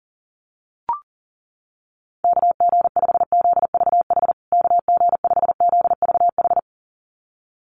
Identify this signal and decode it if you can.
Morse code, heard as a keyed tone, about 40 wpm, 700 Hz. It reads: KG5ZVH KG5ZVH
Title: USA Call Signs Repeated 2x 40wpm